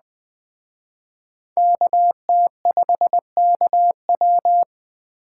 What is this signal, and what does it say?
Morse code, heard as a keyed tone, about 21 wpm, 700 Hz. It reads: KT5KW
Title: Speed Racing - USA Call Signs 17wpm